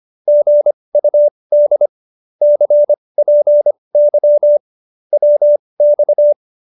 Morse code, heard as a keyed tone, about 25 wpm, spaced as for 20 wpm, 590 Hz. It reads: GUD CPY WX